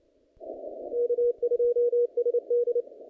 Morse code, heard as a keyed tone, about 28 wpm, 480 Hz. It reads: K2SD
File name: contests / naqp-cw-k5tr-op-k5pi- -0000